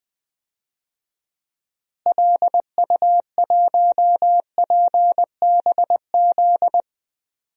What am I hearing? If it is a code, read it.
Morse code, heard as a keyed tone, about 20 wpm, 700 Hz. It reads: LU1PBZ
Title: DXCC-100 Call Signs - 6 Characters - Repeated 2x - 20wpm